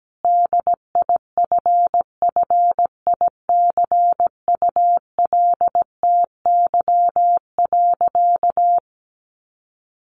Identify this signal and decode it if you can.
Morse code, heard as a keyed tone, about 17 wpm, 700 Hz. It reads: DIFFICULTY.
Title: Sentences from Top 1500 Words 17wpm